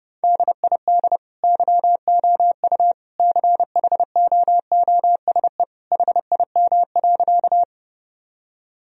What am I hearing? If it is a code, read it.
Morse code, heard as a keyed tone, about 31 wpm, 700 Hz. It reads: DID YOU CHOOSE HIM.
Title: Sentences from Top 1000 Words - Review Part 4 - 30wpm